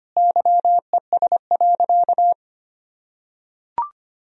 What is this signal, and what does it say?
Morse code, heard as a keyed tone, about 25 wpm, 700 Hz. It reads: YES.
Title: Sentences from Top 1000 Words - Review Part 1 - 25wpm